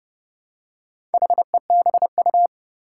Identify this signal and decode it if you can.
Morse code, heard as a keyed tone, about 30 wpm, 700 Hz. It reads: HEBU
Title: Four Challenging Letter-Number Combinations 30wpm